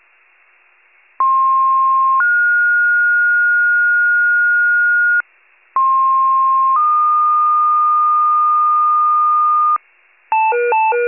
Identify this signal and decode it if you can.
Two-tone page, heard as a one-second tone, then a three-second tone, about 1060 then 1430 Hz, about 1060 then 1220 Hz, with noise. It is heard from 1.2 to 9.8 s.
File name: wcalert